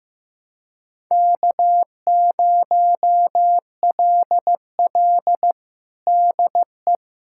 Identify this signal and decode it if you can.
Morse code, heard as a keyed tone, about 15 wpm, 700 Hz. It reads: K0LL DE
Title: USA Callsign DE Callsign - 4 Characters - 15wpm